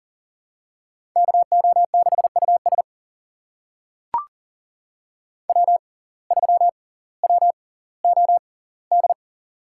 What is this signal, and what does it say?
Morse code, heard as a keyed tone, about 40 wpm, spaced as for 20 wpm, 700 Hz. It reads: KO6US W3WOD
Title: USA Call Signs - 5 Characters - 6x Character Spacing - 40wpm